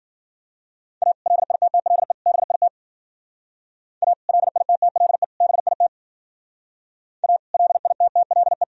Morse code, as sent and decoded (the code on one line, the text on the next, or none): .- / .-.. .. - - .-.. / -... .. - / .- / .-.. .. - - .-.. / -... .. - / .- / .-.. .. - - .-.. .
A LITTLE BIT A LITTLE BIT A LITTLE